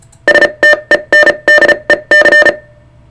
Morse code, heard as a keyed tone, about 34 wpm, spaced as for 27 wpm, 1800 Hz, with noise. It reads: STENDEC
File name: sartechnology / STENDEC 600hz 35WPM